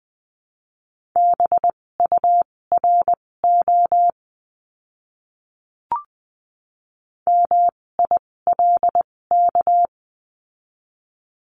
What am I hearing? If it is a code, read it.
Morse code, heard as a keyed tone, about 20 wpm, spaced as for 16 wpm, 700 Hz. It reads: BURO MILK
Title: 256 Lesson 42a - Words 12-20wpm